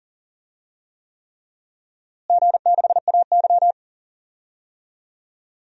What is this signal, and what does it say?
Morse code, heard as a keyed tone, about 40 wpm, 700 Hz. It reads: G6AY
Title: DXCC-100 Call Signs - 4 Characters - Repeated 3x - 40wpm